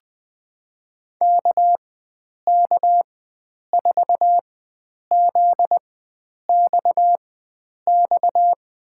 Morse code, heard as a keyed tone, about 21 wpm, spaced as for 9 wpm, 700 Hz. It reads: KK4ZXX
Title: USA Call Signs - 4x Character Spacing - 20wpm